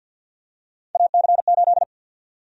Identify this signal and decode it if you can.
Morse code, heard as a keyed tone, about 52 wpm, 700 Hz. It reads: AC8
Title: USA Call Sign Prefixes - 50wpm